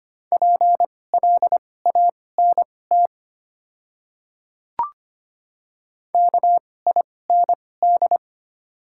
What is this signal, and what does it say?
Morse code, heard as a keyed tone, about 25 wpm, spaced as for 18 wpm, 690 Hz. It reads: PLANT KIND
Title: 2x Character Spacing - Top 300 Words - 25wpm